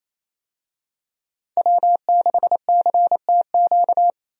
Morse code, heard as a keyed tone, about 27 wpm, 700 Hz. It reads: W6CTQ
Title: USA Call Signs - 5 Characters - 28wpm